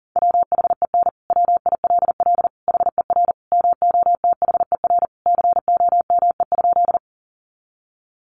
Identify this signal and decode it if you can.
Morse code, heard as a keyed tone, about 40 wpm, 700 Hz. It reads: WHEN WILL HER MOTHER COME?
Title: Sentences from Top 200 Words 40wpm